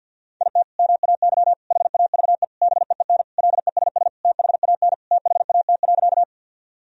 Morse code, heard as a keyed tone, about 50 wpm, 700 Hz. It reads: IT MAY HAVE BEEN LESS THAN THAT.
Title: Sentences from Top 1000 Words - Review Part 2 - 50wpm